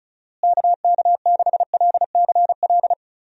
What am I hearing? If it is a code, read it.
Morse code, heard as a keyed tone, about 36 wpm, 700 Hz. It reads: KK6LCL
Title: USA Call Signs - 6 Characters - Repeated 2x - 35wpm